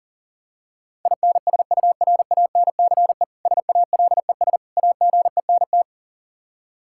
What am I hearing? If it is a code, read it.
Morse code, heard as a keyed tone, about 40 wpm, 700 Hz. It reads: INSURANCE SALES AGENT
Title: K1USN SST QSO Practice 40wpm